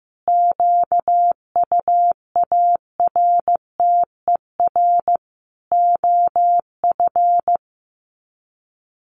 Morse code, heard as a keyed tone, about 15 wpm, 700 Hz. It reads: QUARTER OF
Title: Sets of 5 Words Encore2 15wpm